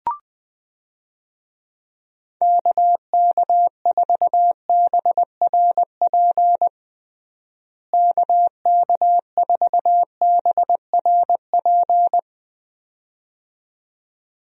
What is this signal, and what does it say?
Morse code, heard as a keyed tone, about 21 wpm, 700 Hz. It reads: KK4BRP KK4BRP
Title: USA Call Signs - 6 Characters - Repeated 2x - 20wpm